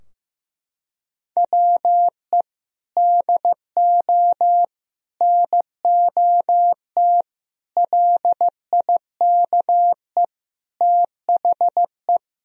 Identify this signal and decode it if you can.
Morse code, heard as a keyed tone, about 16 wpm, 700 Hz. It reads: WE DO NOT LIKE THE
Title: Sentences from Top 1000 Words - Review Part 4 - 15wpm